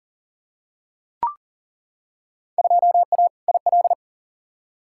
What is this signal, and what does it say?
Morse code, heard as a keyed tone, about 40 wpm, 700 Hz. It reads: 2A IL
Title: Field Day Exchanges - 40wpm